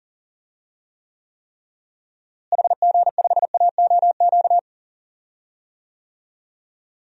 Morse code, heard as a keyed tone, about 37 wpm, 700 Hz. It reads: HG5AOQ
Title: DXCC-100 Call Signs - 6 Characters - Repeated 3x - 40wpm